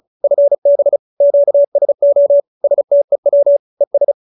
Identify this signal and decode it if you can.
Morse code, heard as a keyed tone, about 34 wpm, 570 Hz. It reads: FB QSO STEW ES